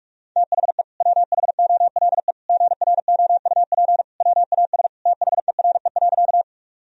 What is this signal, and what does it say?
Morse code, heard as a keyed tone, about 46 wpm, 700 Hz. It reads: THE WHOLE GROUP WAS THERE.
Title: Sentences from Top 1000 Words - Review Part 1 - 45wpm